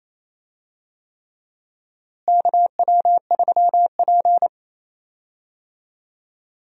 Morse code, heard as a keyed tone, about 27 wpm, 700 Hz. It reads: KW3P
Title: USA Call Signs - 4 Characters - Repeated 3x - 28wpm